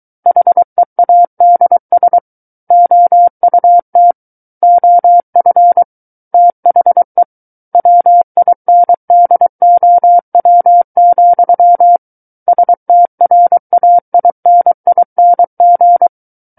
Morse code, heard as a keyed tone, about 23 wpm, 700 Hz. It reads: HEADS OUT OF THE WINDOW, STRAINING
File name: Book One / 14-Chapter 14 - 23 WPM